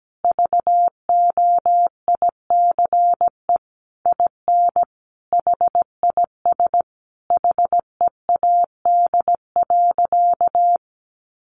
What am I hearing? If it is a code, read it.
Morse code, heard as a keyed tone, about 17 wpm, 700 Hz. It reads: VOICE IN HIS HEAD.
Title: Sentences from Top 1300 Words 17wpm